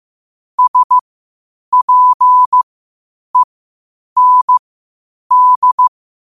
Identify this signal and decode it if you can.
Morse code, heard as a keyed tone, about 15 wpm, spaced as for 8 wpm, 1000 Hz. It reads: SPEND
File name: qotdfiles / 8wpm